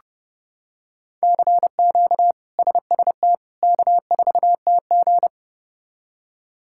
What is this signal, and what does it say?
Morse code, heard as a keyed tone, about 32 wpm, 700 Hz. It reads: CQ SST K4TG
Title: K1USN SST QSO Practice - 30wpm